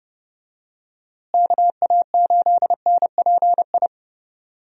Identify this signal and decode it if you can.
Morse code, heard as a keyed tone, about 32 wpm, 700 Hz. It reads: KA8NPI
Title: USA Call Signs - 6 Characters - Repeated 2x - 30wpm